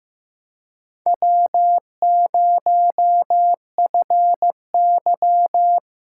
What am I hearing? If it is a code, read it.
Morse code, heard as a keyed tone, about 15 wpm, 700 Hz. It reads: W0FY
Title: USA Callsign DE Callsign - 4 Characters - 15wpm